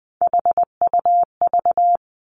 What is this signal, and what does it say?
Morse code, heard as a keyed tone, about 20 wpm, 700 Hz. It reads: HUV
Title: Three Challenging Letter-Number Combinations 20wpm